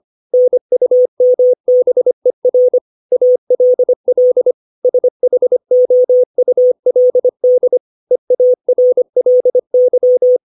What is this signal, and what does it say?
Morse code, heard as a keyed tone, about 25 wpm, 500 Hz. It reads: NUMBER ALL SHOULD EARLY
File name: 632 words / 632 Words 25WPM